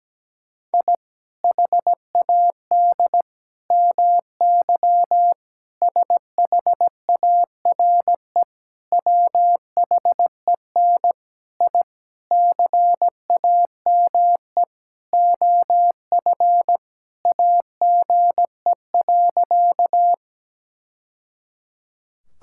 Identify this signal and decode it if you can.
Morse code, heard as a keyed tone, about 17 wpm, 700 Hz. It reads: I HAD MY SHARE WHEN I CAME OF AGE.
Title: Sentences from Top 1000 Words - Review Part 6 - 17wpm